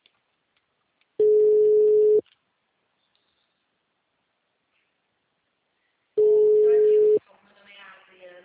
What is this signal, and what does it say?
Ringback tone (Europe, 425 Hz), with speech.
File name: out- -2041- - - .24160